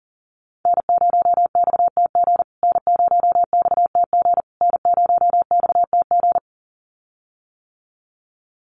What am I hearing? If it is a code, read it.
Morse code, heard as a keyed tone, about 40 wpm, 700 Hz. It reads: N0XTG N0XTG N0XTG
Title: USA Call Signs Repeated 3x 40wpm